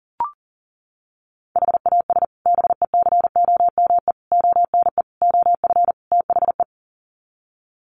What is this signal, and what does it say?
Morse code, heard as a keyed tone, about 37 wpm, 700 Hz. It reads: HAS BECOME ONE OF THE
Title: Sets of 5 Words Encore 40wpm